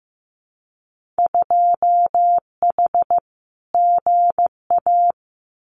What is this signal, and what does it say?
Morse code, heard as a keyed tone, about 15 wpm, 700 Hz. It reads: 2H GA
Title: Winter Field Day Exchanges Encore 15wpm